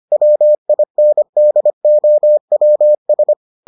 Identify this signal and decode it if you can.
Morse code, heard as a keyed tone, about 25 wpm, 600 Hz. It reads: WINDOWS